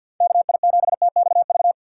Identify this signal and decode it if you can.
Morse code, heard as a keyed tone, about 51 wpm, 700 Hz. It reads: KI7TXV
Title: USA Call Signs - 6 Characters - 4x Character Spacing - 50wpm